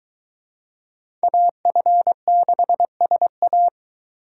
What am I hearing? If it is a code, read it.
Morse code, heard as a keyed tone, about 23 wpm, 700 Hz. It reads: AF6SA